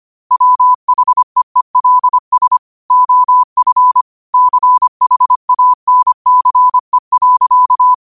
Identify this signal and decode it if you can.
Morse code, heard as a keyed tone, about 25 wpm, 1000 Hz. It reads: WHEELS OF CHANCE.
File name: qotdfiles / 25wpm